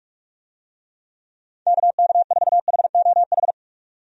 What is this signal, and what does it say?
Morse code, heard as a keyed tone, about 46 wpm, 700 Hz. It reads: KK4HOH
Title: USA Call Signs - 6 Characters - 45wpm